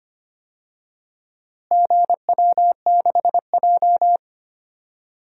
Morse code, heard as a keyed tone, about 25 wpm, 700 Hz. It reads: GW6J
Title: DXCC-100 Call Signs - 4 Characters - Repeated 3x - 25wpm